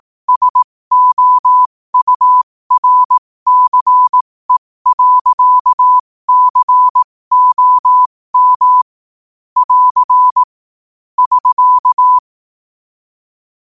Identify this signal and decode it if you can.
Morse code, heard as a keyed tone, about 18 wpm, spaced as for 15 wpm, 1000 Hz. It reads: SOURCE.COM <AR> <SK>
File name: qotdfiles / 15wpm